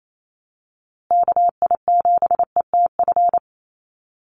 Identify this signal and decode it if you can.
Morse code, heard as a keyed tone, about 28 wpm, 700 Hz. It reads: KI7ETF
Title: USA Call Signs Repeated 3x 28wpm